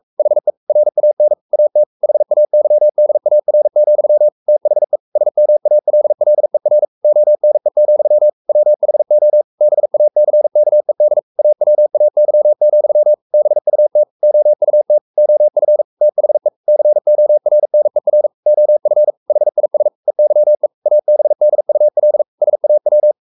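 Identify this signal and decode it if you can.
Morse code, heard as a keyed tone, about 44 wpm, 600 Hz. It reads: HE RAN AT HAYDAR, THE SMALLER ONE, WHO BACKED AWAY, BUT OUT OF THE CORNER OF HIS EYE ABDUL SAW
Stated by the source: Morse code at 43 wpm sending Ken Follett's novel Never in English, from CW med Gnister.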